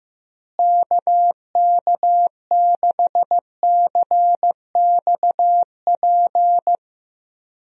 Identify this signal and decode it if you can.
Morse code, heard as a keyed tone, about 16 wpm, 700 Hz. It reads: KK6CXP
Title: USA Call Signs - 6 Characters - Repeated 2x - 15wpm